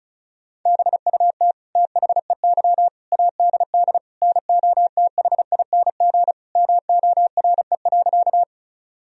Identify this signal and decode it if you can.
Morse code, heard as a keyed tone, about 34 wpm, 700 Hz. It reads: BUT THEY ADD NOTHING MORE.
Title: Sentences from Top 1000 Words - Review Part 1 - 35wpm